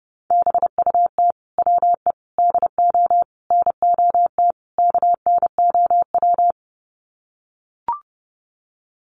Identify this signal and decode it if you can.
Morse code, heard as a keyed tone, about 30 wpm, 700 Hz. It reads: BUT WE DO NOT KNOW
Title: Sets of 5 Words Encore2 30wpm